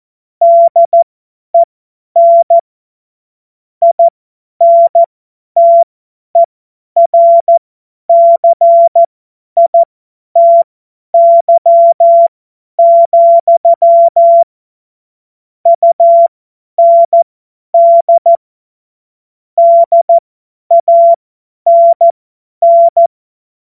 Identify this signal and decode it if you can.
Morse code, heard as a keyed tone, about 14 wpm, spaced as for 10 wpm, 680 Hz. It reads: DEN INTERCITY, UND DANN
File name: Morsekurs / Witze / Witz 02 Januar 2024 50 F70